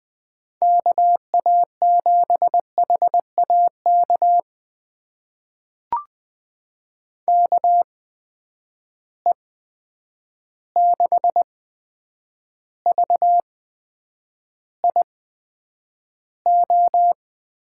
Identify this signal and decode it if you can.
Morse code, heard as a keyed tone, about 20 wpm, spaced as for 9 wpm, 700 Hz. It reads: KA7HAK KE6VIO
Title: USA Call Signs - 6 Characters - 8x Character Spacing - 20wpm